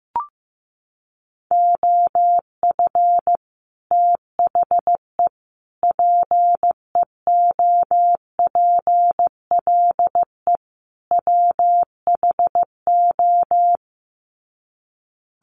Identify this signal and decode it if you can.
Morse code, heard as a keyed tone, about 15 wpm, 700 Hz. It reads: OF THE PEOPLE WHO
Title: Sets of 4 Words 15wpm